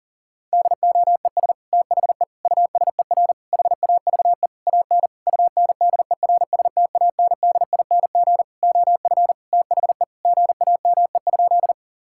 Morse code, heard as a keyed tone, about 40 wpm, 700 Hz. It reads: DOES THE USER HAVE AN UNDERSTANDING OF THE GAME?
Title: Sentences from Top 1000 Words - Review Part 5 - 40wpm